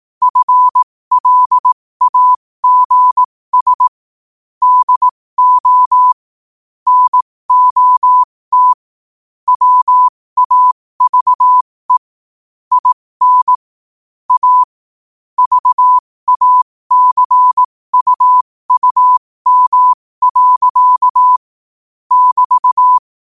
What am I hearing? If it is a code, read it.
Morse code, heard as a keyed tone, about 18 wpm, spaced as for 15 wpm, 1000 Hz. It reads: FLAGS DO NOT WAVE IN A VACUUM. <BT>
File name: qotdfiles / 15wpm